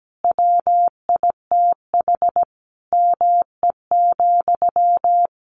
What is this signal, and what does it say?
Morse code, heard as a keyed tone, about 17 wpm, 700 Hz. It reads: WITH ME,
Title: Sentences from Top 200 Words 17wpm